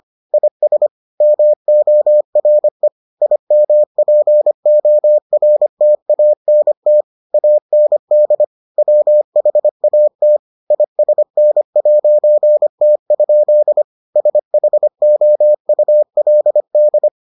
Morse code, heard as a keyed tone, about 25 wpm, 600 Hz. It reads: IS MORE IMPORTANT AND WHAT ISN'T? SHOULD